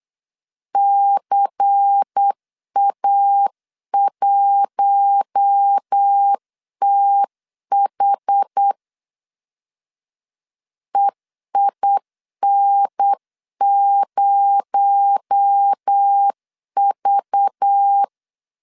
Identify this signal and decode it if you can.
Morse code, heard as a keyed tone, about 8 wpm, 800 Hz. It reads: CA1TH EIN0V
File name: CW / DET / 1 6 1500C S C 8wpm 5219